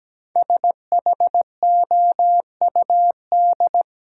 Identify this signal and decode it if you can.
Morse code, heard as a keyed tone, about 17 wpm, 700 Hz. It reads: SHOUD
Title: Sentences from Top 1000 Words - Review Part 4 - New Word Drill - 17wpm